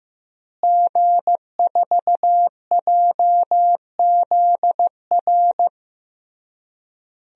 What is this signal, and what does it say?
Morse code, heard as a keyed tone, about 16 wpm, 700 Hz. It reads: G4JZR
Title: DXCC-100 Call Signs - 5 Characters - Repeated 3x - 15wpm